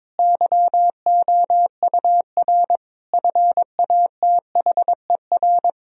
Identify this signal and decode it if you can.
Morse code, heard as a keyed tone, about 22 wpm, 700 Hz. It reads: YOUR FATHER
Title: Sentences from Top 1000 Words - Review Part 1 - 22wpm